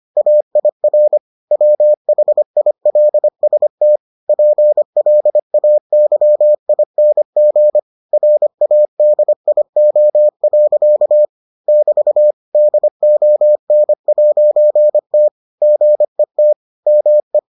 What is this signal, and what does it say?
Morse code, heard as a keyed tone, about 25 wpm, 600 Hz. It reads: AIR WHILST PLAYING RADIO. <BT> DON'T GET ME